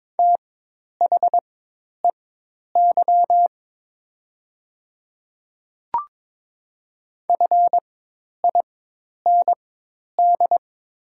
Morse code, heard as a keyed tone, about 23 wpm, spaced as for 10 wpm, 700 Hz. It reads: THEY FIND
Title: 4x Character Spacing - Top 100 Words - 22wpm